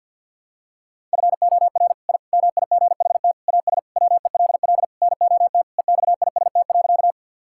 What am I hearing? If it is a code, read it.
Morse code, heard as a keyed tone, about 53 wpm, 700 Hz. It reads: FOR I MIGHT AS WELL NOT EXIST.
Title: Sentences from Top 1000 Words - Review Part 5 - 50wpm